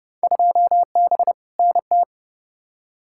Morse code, heard as a keyed tone, about 29 wpm, 700 Hz. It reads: 2B NT
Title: Field Day Exchanges Encore - 30wpm